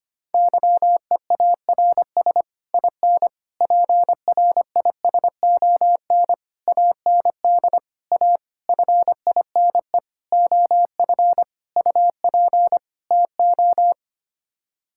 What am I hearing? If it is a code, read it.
Morse code, heard as a keyed tone, about 25 wpm, 700 Hz. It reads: YEARS IN PRISON AND A FINE OF UP TO
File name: Sets of 9 Words - 25wpm